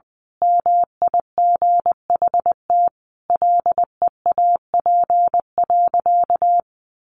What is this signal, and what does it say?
Morse code, heard as a keyed tone, about 20 wpm, 700 Hz. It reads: MIGHT LEAP.
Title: A Sound Of Thunder 20wpm